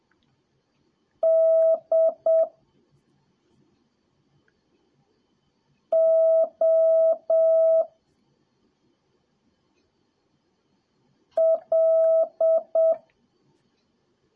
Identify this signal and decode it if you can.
Morse code, heard as a keyed tone, about 7 wpm, 1300 Hz. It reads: D O L
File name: jh8cbh / 2021cw / cw03a